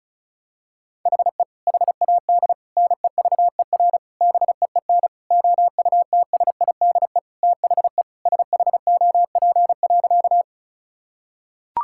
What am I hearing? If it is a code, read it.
Morse code, heard as a keyed tone, about 36 wpm, 700 Hz. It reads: HE HAD NEVER BEEN OUTSIDE THE SHOP.
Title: Sentences from Top 1600 Words 35wpm